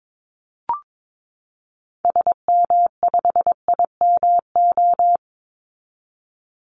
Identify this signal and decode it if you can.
Morse code, heard as a keyed tone, about 22 wpm, 700 Hz. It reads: SM5IMO
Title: Call Signs Repeated 3x - 22wpm